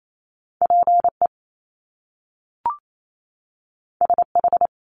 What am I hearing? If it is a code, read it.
Morse code, heard as a keyed tone, about 28 wpm, 700 Hz. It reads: PE SH
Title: Top Two Letter Combinations in Words 28wpm